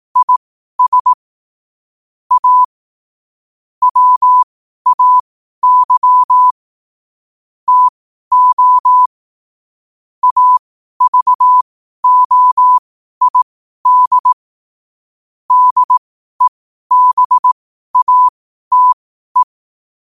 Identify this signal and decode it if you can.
Morse code, heard as a keyed tone, about 18 wpm, spaced as for 12 wpm, 1000 Hz. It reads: IS A WAY TO AVOID DEBATE
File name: qotdfiles / 12wpm